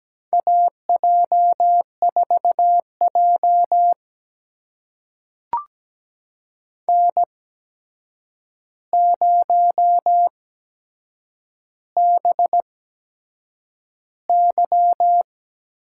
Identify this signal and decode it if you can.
Morse code, heard as a keyed tone, about 17 wpm, spaced as for 7 wpm, 700 Hz. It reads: AJ4J N0BY
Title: USA Call Signs - 4 Characters - 8x Character Spacing - 17wpm